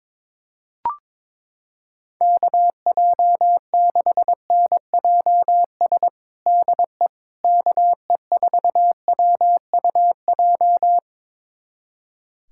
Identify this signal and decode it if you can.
Morse code, heard as a keyed tone, about 22 wpm, 700 Hz. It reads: KJ6NJS DE KE4WUJ
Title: USA Callsign DE Callsign 22wpm